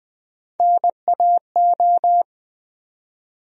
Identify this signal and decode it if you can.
Morse code, heard as a keyed tone, about 20 wpm, 700 Hz. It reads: NAO
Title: USA Call Sign Postfixes - 3 Letters 20wpm